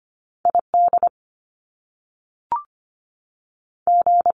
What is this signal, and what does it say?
Morse code, heard as a keyed tone, about 25 wpm, 700 Hz. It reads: ID G
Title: US States 25wpm